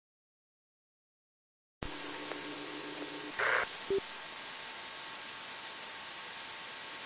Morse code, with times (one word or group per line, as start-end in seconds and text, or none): none